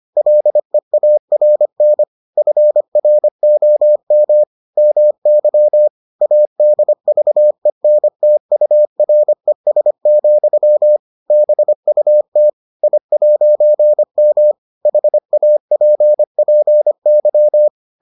Morse code, heard as a keyed tone, about 25 wpm, 600 Hz. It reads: LEARN FROM MY ADVENTURES, BUT I'M HAPPY